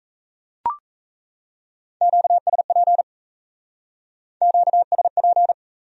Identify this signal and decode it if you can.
Morse code, heard as a keyed tone, about 37 wpm, 700 Hz. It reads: QSP QSP